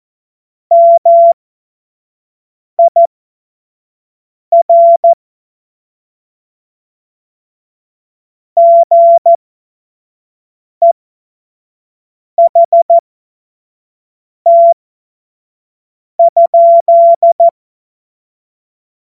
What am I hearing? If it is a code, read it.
Morse code, heard as a keyed tone, about 14 wpm, spaced as for 5 wpm, 680 Hz. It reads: MIR GEHT?
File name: Morsekurs / Witze / Witz 10. Mai 2022 25 F70